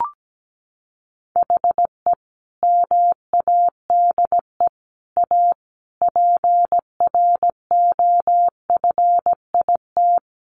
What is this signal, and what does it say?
Morse code, heard as a keyed tone, about 17 wpm, 700 Hz. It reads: HE MADE A PROFIT
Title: Sentences from Top 800 Words 17wpm